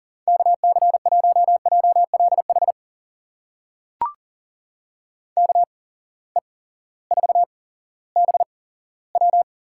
Morse code, heard as a keyed tone, about 40 wpm, spaced as for 18 wpm, 700 Hz. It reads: KC1JLH KE4BW
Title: USA Call Signs - 6 Characters - 8x Character Spacing - 40wpm